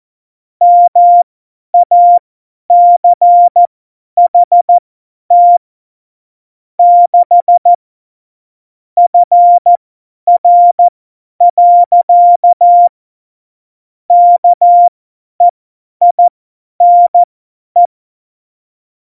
Morse code, heard as a keyed tone, about 14 wpm, spaced as for 10 wpm, 700 Hz. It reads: MACHT 6 FR. KEINE
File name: Morsekurs / Witze / Witz 05. Februar 2019, 50 F70